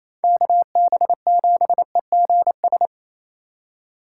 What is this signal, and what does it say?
Morse code, heard as a keyed tone, about 28 wpm, 690 Hz. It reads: KB7EGS
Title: USA Call Signs - 2x Character Spacing - 28wpm